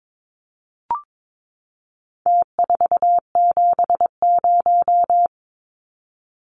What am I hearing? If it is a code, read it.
Morse code, heard as a keyed tone, about 22 wpm, 700 Hz. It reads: T470
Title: Numbers - Contest Style - 4-Digit 22wpm